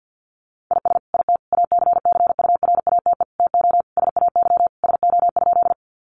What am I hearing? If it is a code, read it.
Morse code, heard as a keyed tone, about 49 wpm, 700 Hz. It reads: IS IT ACCURATE TO SAY SO?
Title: Sentences from Top 1200 Words 50wpm